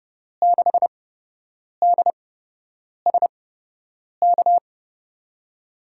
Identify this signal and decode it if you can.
Morse code, heard as a keyed tone, about 31 wpm, spaced as for 8 wpm, 700 Hz. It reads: 6DSK
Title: USA Call Signs - 6 Characters - 8x Character Spacing - 30wpm